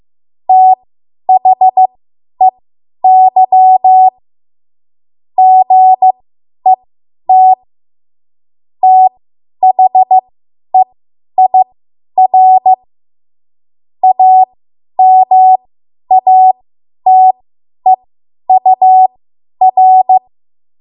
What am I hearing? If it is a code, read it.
Morse code, heard as a keyed tone, about 15 wpm, spaced as for 10 wpm, 750 Hz. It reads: THEY GET THEIR AMATEUR
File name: Morse / 10WPM